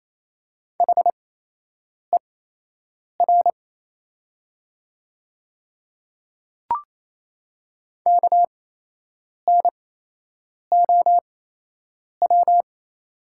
Morse code, heard as a keyed tone, about 28 wpm, spaced as for 8 wpm, 700 Hz. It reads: HER KNOW
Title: 8x Character Spacing - Top 200 Words - 28wpm